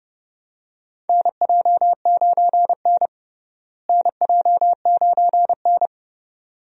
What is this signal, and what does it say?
Morse code, heard as a keyed tone, about 30 wpm, 700 Hz. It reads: NJ9N NJ9N
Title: USA Call Signs - 4 Characters - Repeated 2x - 30wpm